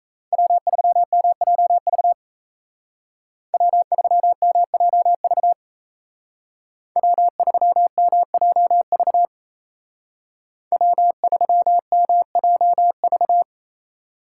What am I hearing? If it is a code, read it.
Morse code, heard as a keyed tone, about 33 wpm, 700 Hz. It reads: W3MJV W3MJV W3MJV W3MJV
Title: Speed Racing - USA 1x3 Call Signs 28wpm